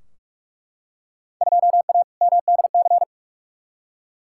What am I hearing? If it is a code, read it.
Morse code, heard as a keyed tone, about 43 wpm, 700 Hz. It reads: 2A MDC
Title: Field Day Exchanges - 45wpm